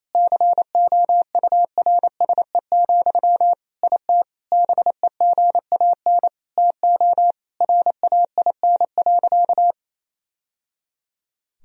Morse code, heard as a keyed tone, about 28 wpm, 700 Hz. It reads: COURSE, IT BEGAN TO RAIN.
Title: Sentences from Top 1000 Words - Review Part 1 - 28wpm